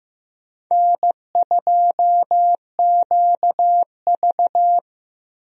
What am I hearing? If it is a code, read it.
Morse code, heard as a keyed tone, about 15 wpm, 700 Hz. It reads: N2QV
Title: USA Call Signs - 4 Characters - Repeated 2x - 15wpm